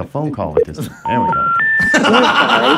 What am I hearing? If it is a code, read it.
Special information tone, with speech.